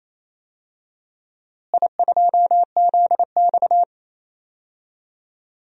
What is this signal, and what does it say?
Morse code, heard as a keyed tone, about 28 wpm, 700 Hz. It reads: I2ZX
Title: DXCC-100 Call Signs - 4 Characters - Repeated 2x - 28wpm